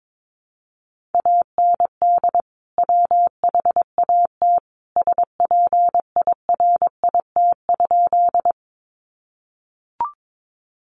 Morse code, heard as a keyed tone, about 22 wpm, 700 Hz. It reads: AND WHAT SPIRIT?
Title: Sentences from Top 1400 Words 22wpm